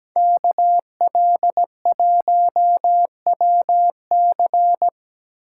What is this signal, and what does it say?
Morse code, heard as a keyed tone, about 20 wpm, 700 Hz. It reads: KL1WC
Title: Speed Racing - USA 2x2 Call Signs 17wpm